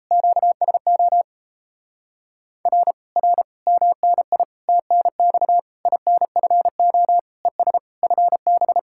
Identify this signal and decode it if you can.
Morse code, heard as a keyed tone, about 37 wpm, 700 Hz. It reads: QSO R R MNI TNX INFO ES FB